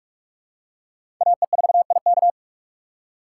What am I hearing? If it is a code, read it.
Morse code, heard as a keyed tone, about 48 wpm, 700 Hz. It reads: AE4IK
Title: USA Call Signs - 5 Characters - 2x Character Spacing - 45wpm